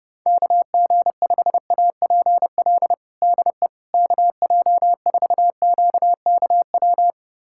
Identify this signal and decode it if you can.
Morse code, heard as a keyed tone, about 30 wpm, 700 Hz. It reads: KG5APL DE KJ4QKW
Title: USA Callsign DE Callsign 30wpm